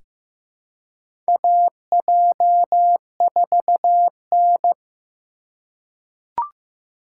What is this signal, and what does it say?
Morse code, heard as a keyed tone, about 15 wpm, 700 Hz. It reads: AJ4N
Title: USA Call Signs - 4 Characters - 4x Character Spacing - 15wpm